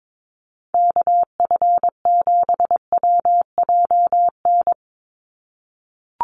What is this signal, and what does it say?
Morse code, heard as a keyed tone, about 22 wpm, 700 Hz. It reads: KF7WJN
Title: USA Call Signs 22wpm